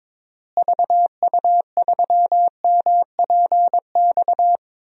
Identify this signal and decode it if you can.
Morse code, heard as a keyed tone, about 22 wpm, 700 Hz. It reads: VU3MPX
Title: DXCC-100 Call Signs - 6 Characters - Repeated 2x - 22wpm